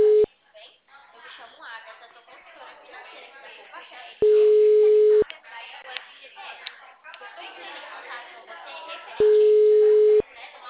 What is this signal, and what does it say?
Ringback tone (Europe, 425 Hz), with speech.